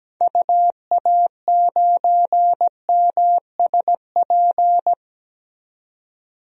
Morse code, heard as a keyed tone, about 17 wpm, 700 Hz. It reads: UA9MSP
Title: DXCC-100 Call Signs - 6 Characters - 17wpm